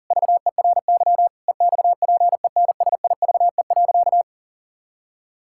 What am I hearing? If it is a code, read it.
Morse code, heard as a keyed tone, about 40 wpm, 700 Hz. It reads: VERY EXPENSIVE.
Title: Sentences from Top 1000 Words - Review Part 6 - 40wpm